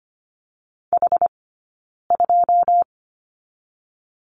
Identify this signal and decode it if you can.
Morse code, heard as a keyed tone, about 25 wpm, 700 Hz. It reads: H 2
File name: Lesson 15d - Number 5 - Two Characters 25wpm